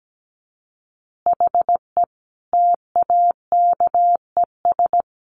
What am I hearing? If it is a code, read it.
Morse code, heard as a keyed tone, about 17 wpm, 700 Hz. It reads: HE TAKES
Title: Additional Easy Phrases 17wpm